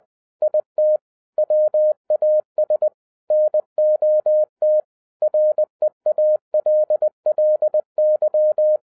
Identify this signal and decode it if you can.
Morse code, heard as a keyed tone, about 20 wpm, 610 Hz. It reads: IT WAS NOT REALLY
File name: 2011-07-mp3 / n58 BACK FROM A BIG ACCIDENT 20wpm